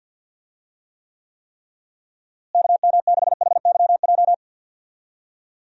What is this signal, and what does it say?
Morse code, heard as a keyed tone, about 48 wpm, 700 Hz. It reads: KM6HYJ
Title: USA Call Signs - 6 Characters - Repeated 3x - 50wpm